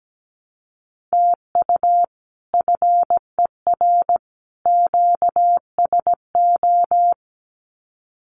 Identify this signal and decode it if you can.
Morse code, heard as a keyed tone, about 17 wpm, 700 Hz. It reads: TU FER QSO
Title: Common QSO Phrases 17wpm